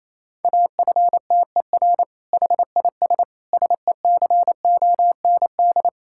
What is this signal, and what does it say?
Morse code, heard as a keyed tone, about 28 wpm, 700 Hz. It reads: AFTER HIS SECOND
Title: Sentences from Top 1000 Words - Review Part 6 - 28wpm